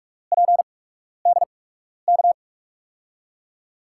Morse code, heard as a keyed tone, about 42 wpm, spaced as for 12 wpm, 700 Hz. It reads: PDK
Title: USA Call Signs - 5 Characters - 8x Character Spacing - 45wpm